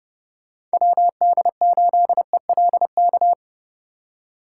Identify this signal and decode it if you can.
Morse code, heard as a keyed tone, about 32 wpm, 700 Hz. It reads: WD8ELK